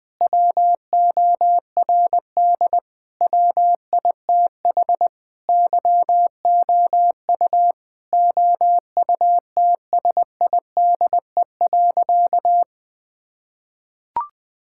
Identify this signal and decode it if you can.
Morse code, heard as a keyed tone, about 20 wpm, 700 Hz. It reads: WORD WITH YOU OUTSIDE.
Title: Sentences from Top 1000 Words - Review Part 6 - 20wpm